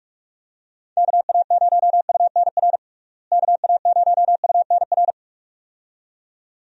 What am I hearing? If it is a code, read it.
Morse code, heard as a keyed tone, about 46 wpm, 700 Hz. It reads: KA0UNR KA0UNR
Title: USA Call Signs - 6 Characters - Repeated 2x - 45wpm